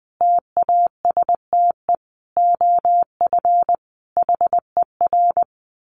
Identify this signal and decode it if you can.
Morse code, heard as a keyed tone, about 20 wpm, 700 Hz. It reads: TASTE OF HER
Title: Sentences from Top 1100 Words 20wpm